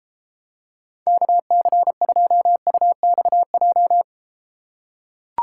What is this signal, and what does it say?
Morse code, heard as a keyed tone, about 33 wpm, 700 Hz. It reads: KC2UXJ